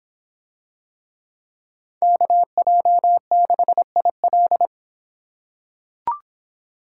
Morse code, heard as a keyed tone, about 26 wpm, 700 Hz. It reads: KJ6IL